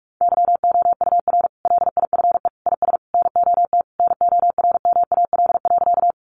Morse code, heard as a keyed tone, about 44 wpm, 700 Hz. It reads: YOUR LIFE IS NOT NORMAL.
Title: Sentences from Top 1200 Words 45wpm